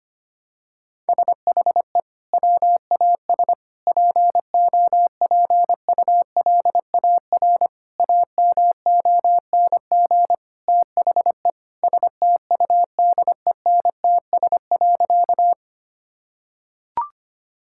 Morse code, heard as a keyed tone, about 25 wpm, 700 Hz. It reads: SHE WAS POPULAR AMONG THE STUDENTS.
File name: practice / 25wpm / Sentences with 2-Word Phrases - 25wpm